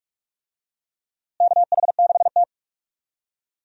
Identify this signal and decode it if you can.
Morse code, heard as a keyed tone, about 43 wpm, 700 Hz. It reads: KH6T
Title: DXCC-100 Call Signs - 4 Characters - Repeated 2x - 45wpm